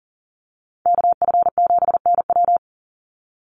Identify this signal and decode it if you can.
Morse code, heard as a keyed tone, about 40 wpm, 700 Hz. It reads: KF7NW
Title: USA Call Signs 40wpm